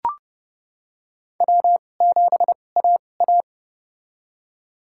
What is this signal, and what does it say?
Morse code, heard as a keyed tone, about 32 wpm, spaced as for 22 wpm, 700 Hz. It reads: W7AA
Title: USA Call Signs - 4 Characters - 2x Character Spacing - 30wpm